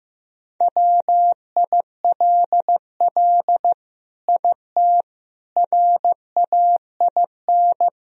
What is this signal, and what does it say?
Morse code, heard as a keyed tone, about 15 wpm, 700 Hz. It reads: WILL IT RAIN
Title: Sentences from Top 1000 Words - Review Part 1 - 15wpm